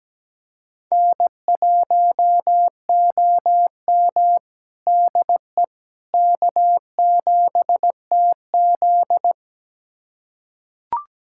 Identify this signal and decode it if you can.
Morse code, heard as a keyed tone, about 17 wpm, 700 Hz. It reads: N1OM DE K7TZ
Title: USA Callsign DE Callsign - 4 Characters - 17wpm